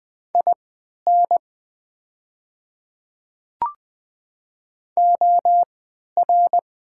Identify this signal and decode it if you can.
Morse code, heard as a keyed tone, about 21 wpm, spaced as for 12 wpm, 700 Hz. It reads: IN OR
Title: 3x Character Spacing - Top 100 Words - 20wpm